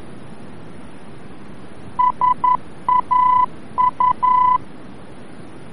Morse code, heard as a keyed tone, about 11 wpm, 1000 Hz. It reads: SAU
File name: france / mx / SAU-VOR-F